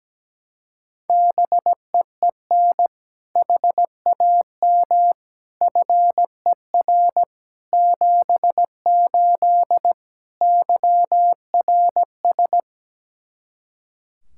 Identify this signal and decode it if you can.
Morse code, heard as a keyed tone, about 17 wpm, 700 Hz. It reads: BEEN HAM FER 78 YRS
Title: Practice QSO - Exchange 3 - More Info - 17wpm